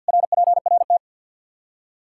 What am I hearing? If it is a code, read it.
Morse code, heard as a keyed tone, about 47 wpm, 700 Hz. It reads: RPRT
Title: Practice QSO - Exchange 2 - Info - 50wpm